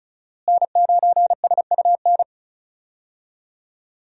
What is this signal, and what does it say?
Morse code, heard as a keyed tone, about 36 wpm, 700 Hz. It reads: N9SUN
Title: USA Call Signs - 5 Characters - Repeated 3x - 35wpm